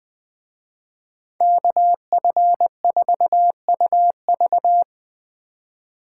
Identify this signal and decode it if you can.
Morse code, heard as a keyed tone, about 20 wpm, 700 Hz. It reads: KF4UV
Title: Speed Racing - USA 2x2 Call Signs 17wpm